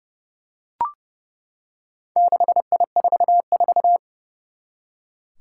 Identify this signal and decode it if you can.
Morse code, heard as a keyed tone, about 30 wpm, 700 Hz. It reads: 6I44
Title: Four Challenging Letter-Number Combinations 30wpm